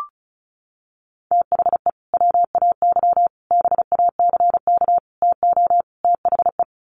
Morse code, heard as a keyed tone, about 34 wpm, 700 Hz. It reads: THE WAY BACK TO THE
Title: Sets of 5 Words Encore2 35wpm